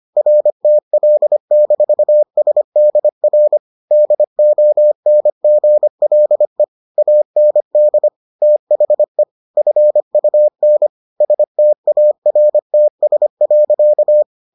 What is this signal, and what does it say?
Morse code, heard as a keyed tone, about 25 wpm, 600 Hz. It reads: RTL-SDR DONGLE AND THE FUN STARTS.